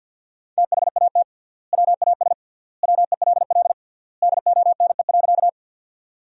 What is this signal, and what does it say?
Morse code, heard as a keyed tone, about 52 wpm, 700 Hz. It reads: THAT WAS WELL DONE.
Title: 3x Word-Spacing - Sentences from Top 500 Words - 50wpm